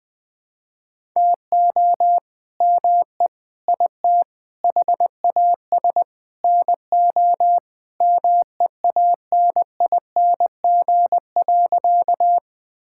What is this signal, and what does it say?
Morse code, heard as a keyed tone, about 20 wpm, 700 Hz. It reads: TO ME IT HAS NO MEANING.
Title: Sentences from Top 1000 Words - Review Part 6 - 20wpm